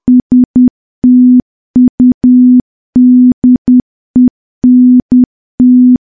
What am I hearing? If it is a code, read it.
Morse code, heard as a keyed tone, about 10 wpm, 250 Hz. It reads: STUDENT
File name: morse / morse-training-10WPM